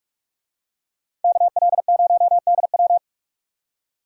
Morse code, heard as a keyed tone, about 42 wpm, 700 Hz. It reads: KL0DW